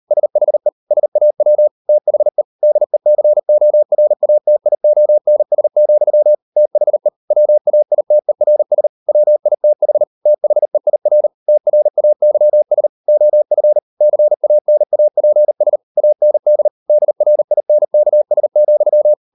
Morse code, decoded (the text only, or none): SHE SAW THE DECORATIONS, THE WAITERS WITH THEIR TRAYS OF CANAPS AND DRINKS,